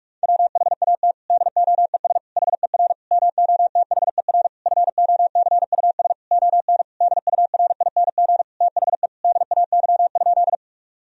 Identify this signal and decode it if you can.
Morse code, heard as a keyed tone, about 41 wpm, 700 Hz. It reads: WHAT DOES HER MOTHER FOCUS ON DURING THE DAY?
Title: Sentences from Top 1000 Words - Review Part 4 - 45wpm